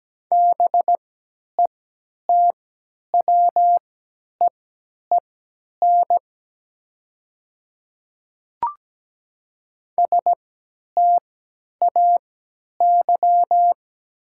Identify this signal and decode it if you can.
Morse code, heard as a keyed tone, about 17 wpm, spaced as for 10 wpm, 690 Hz. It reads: BETWEEN STAY
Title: 3x Character Spacing - Top 500 Words - 17wpm